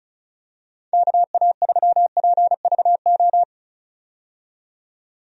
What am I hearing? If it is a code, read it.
Morse code, heard as a keyed tone, about 36 wpm, 700 Hz. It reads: KA3PVO
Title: USA Call Signs - 6 Characters - 35wpm